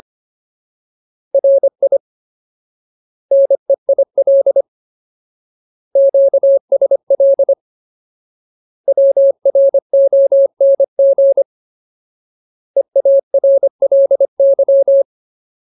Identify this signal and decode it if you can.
Morse code, heard as a keyed tone, about 25 wpm, 550 Hz. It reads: RI NEIL QSL WRONG EARLY
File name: Ultimate Word List / 25WPM Ultimate